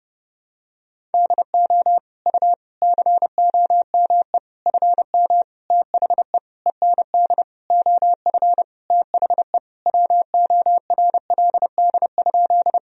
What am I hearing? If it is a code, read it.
Morse code, heard as a keyed tone, about 32 wpm, 700 Hz. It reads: DO U COME FM THE END OF THE WORLD?
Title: Sentences from Top 200 Words and Abbreviations 30wpm